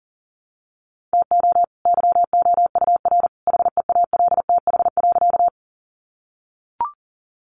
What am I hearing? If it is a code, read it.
Morse code, heard as a keyed tone, about 40 wpm, 700 Hz. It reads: TO YOUR HEALTH.
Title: Sentences from Top 600 Words 40wpm